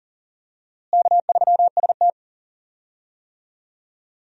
Morse code, heard as a keyed tone, about 37 wpm, 700 Hz. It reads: K3ST
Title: USA Call Signs - 4 Characters - Repeated 2x - 40wpm